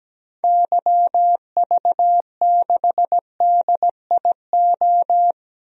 Morse code, heard as a keyed tone, about 17 wpm, 700 Hz. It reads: YV6DIO
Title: DXCC-100 Call Signs - 6 Characters - 17wpm